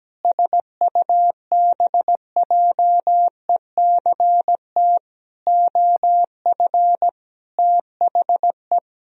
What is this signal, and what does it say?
Morse code, heard as a keyed tone, about 17 wpm, 700 Hz. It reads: SUBJECT OF THE
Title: Sentences from Top 1000 Words - Review Part 1 - 17wpm